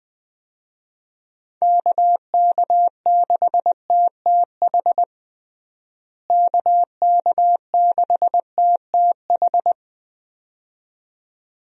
Morse code, heard as a keyed tone, about 20 wpm, 700 Hz. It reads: KK6TTH KK6TTH
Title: USA Call Signs - 6 Characters - Repeated 2x - 20wpm